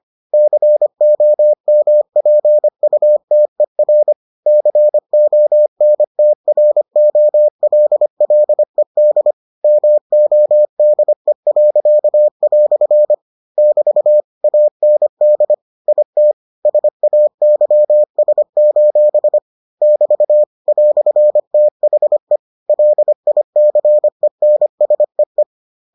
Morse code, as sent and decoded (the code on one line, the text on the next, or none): -.-. --- -- .--. ..- - . .-. / -.-. --- -. - .-. --- .-.. .-.. -.. / -- --- -.. .-.-.- .-..-. / -...- / .- -. -.. / .. - / ... .- -.-- ... ---... / -...- / .-..-. - .... / .-.. .. -.-. -. ... .
COMPUTER CONTROLLED MODE." <BT> AND IT SAYS: <BT> "THE LICENSEE